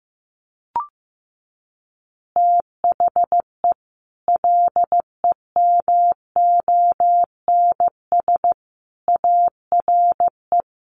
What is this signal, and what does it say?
Morse code, heard as a keyed tone, about 15 wpm, 700 Hz. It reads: THE LEMONS ARE
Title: Additional Easy Phrases 15wpm